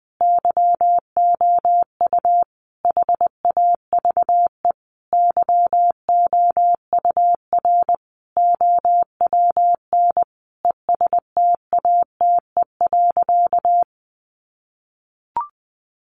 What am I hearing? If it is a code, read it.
Morse code, heard as a keyed tone, about 19 wpm, 700 Hz. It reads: YOU HAVE YOUR OWN ESTATE.
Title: Sentences from Top 1200 Words 20wpm